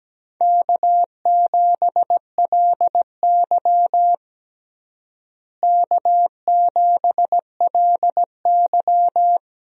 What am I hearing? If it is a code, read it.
Morse code, heard as a keyed tone, about 17 wpm, 700 Hz. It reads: K7LY K7LY
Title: USA Call Signs - 4 Characters - Repeated 3x - 17wpm